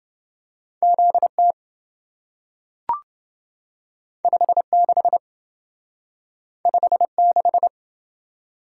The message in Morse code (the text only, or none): ZT 56 56